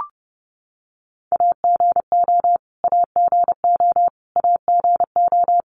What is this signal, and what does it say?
Morse code, heard as a keyed tone, about 30 wpm, 700 Hz. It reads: AGO AGO AGO
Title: Top 500 Words Repeated 30wpm